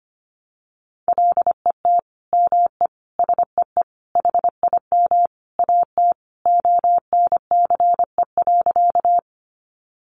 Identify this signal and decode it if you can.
Morse code, heard as a keyed tone, about 25 wpm, 700 Hz. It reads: LET ME SEE HIM AT ONCE.
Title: Sentences from Top 300 Words 25wpm